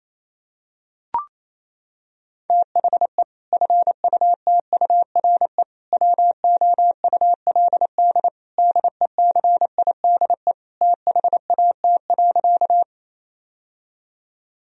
Morse code, heard as a keyed tone, about 27 wpm, 700 Hz. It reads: THE FUTURE WOULD DECIDE THAT.
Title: Sentences from Top 1000 Words - Review Part 3 - 28wpm